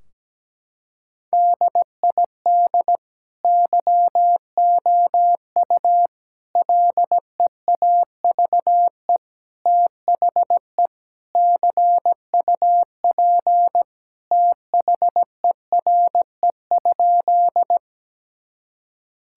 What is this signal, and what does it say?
Morse code, heard as a keyed tone, about 17 wpm, 700 Hz. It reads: DID YOU LEAVE THE CUP THERE?
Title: Sentences from Top 1600 Words 17wpm